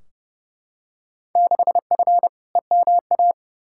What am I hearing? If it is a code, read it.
Morse code, heard as a keyed tone, about 30 wpm, 700 Hz. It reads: 6F EMA
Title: Field Day Exchanges Encore - 30wpm